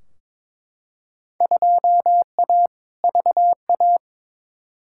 Morse code, heard as a keyed tone, about 22 wpm, 700 Hz. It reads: 2A VA
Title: Field Day Exchanges - 22wpm